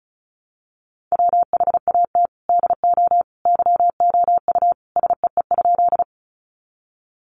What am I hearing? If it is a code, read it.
Morse code, heard as a keyed tone, about 34 wpm, 700 Hz. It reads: WHAT DO YOU SEE?
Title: Sentences from Top 100 Words 35wpm